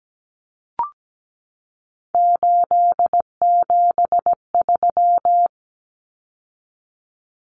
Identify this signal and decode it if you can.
Morse code, heard as a keyed tone, about 17 wpm, 700 Hz. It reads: 873
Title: Common Cut Numbers - 3-Digit 17wpm